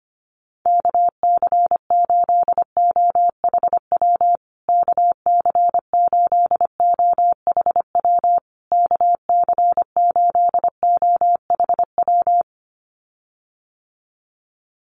Morse code, heard as a keyed tone, about 25 wpm, 700 Hz. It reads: KC8OHW KC8OHW KC8OHW
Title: USA Call Signs Repeated 3x 25wpm